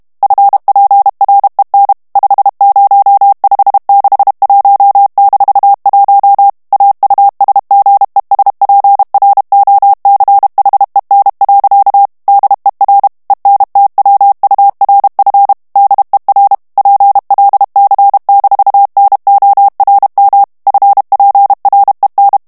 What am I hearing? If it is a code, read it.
Morse code, heard as a keyed tone, about 32 wpm, 800 Hz. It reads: FPREN 50561-1 AUSGESPROCHEN. DER ENTWURF DER PLC-NORM FPREN
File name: ALLEMAND texte clair 1 vitesse 30 mots minute